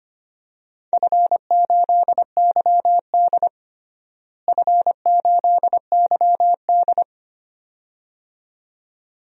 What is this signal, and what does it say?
Morse code, heard as a keyed tone, about 25 wpm, 700 Hz. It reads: F8YD F8YD
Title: DXCC-100 Call Signs - 4 Characters - Repeated 2x - 25wpm